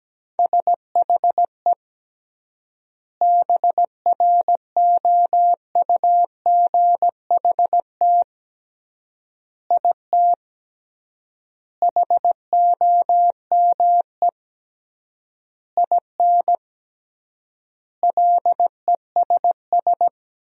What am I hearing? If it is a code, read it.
Morse code, heard as a keyed tone, about 17 wpm, 700 Hz. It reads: SHE BROUGHT IT HOME IN LESS
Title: 3x Word-Spacing - Sentences from Top 500 Words - 17wpm